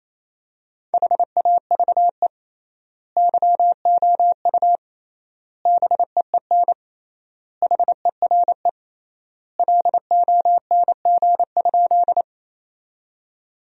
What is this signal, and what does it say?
Morse code, heard as a keyed tone, about 28 wpm, 700 Hz. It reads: HAVE YOU BEEN HERE LONG?
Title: 3x Word-Spacing - Sentences from Top 200 Words - 28wpm